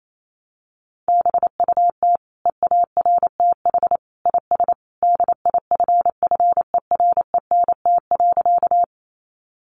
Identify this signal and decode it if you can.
Morse code, heard as a keyed tone, about 28 wpm, 700 Hz. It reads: BUT EARTH IS DIFFERENT.
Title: Sentences from Top 600 Words 28wpm